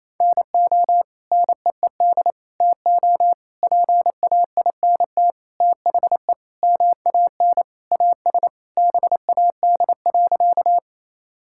Morse code, heard as a keyed tone, about 28 wpm, 700 Hz. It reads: NO NEED TO PAINT THE MAN AS BAD.
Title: Sentences from Top 1000 Words - Review Part 6 - 28wpm